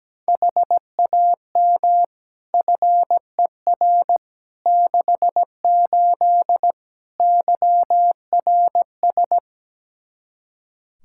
Morse code, heard as a keyed tone, about 17 wpm, 700 Hz. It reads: HAM FER 68 YRS
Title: K1USN SST QSO Practice 17wpm